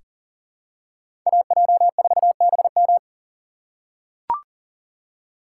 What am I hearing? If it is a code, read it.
Morse code, heard as a keyed tone, about 39 wpm, 700 Hz. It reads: AJ4BM
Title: USA Call Signs - 5 Characters - 6x Character Spacing - 40wpm